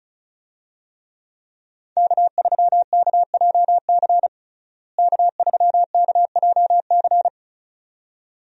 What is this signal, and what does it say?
Morse code, heard as a keyed tone, about 36 wpm, 700 Hz. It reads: K3KJC K3KJC
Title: USA Call Signs - 5 Characters - Repeated 2x - 35wpm